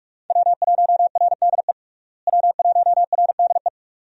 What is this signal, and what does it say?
Morse code, heard as a keyed tone, about 41 wpm, 700 Hz. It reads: W1RDE W1RDE
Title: USA Call Signs - 5 Characters - Repeated 2x - 45wpm